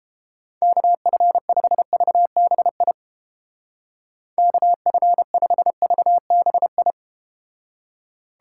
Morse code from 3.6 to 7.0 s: KF5VBI

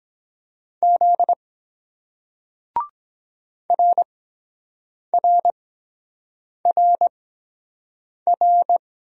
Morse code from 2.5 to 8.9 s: RRRR

